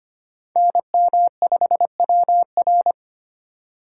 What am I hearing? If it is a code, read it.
Morse code, heard as a keyed tone, about 34 wpm, spaced as for 30 wpm, 700 Hz. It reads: NM5WR
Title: Speed Racing - USA Call Signs 25wpm